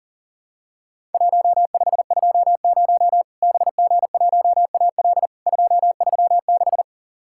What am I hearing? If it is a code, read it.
Morse code, heard as a keyed tone, about 44 wpm, 700 Hz. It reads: 1520 BG1AL 236
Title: Worked All Europe DX Contest - QTC Exchange - 40wpm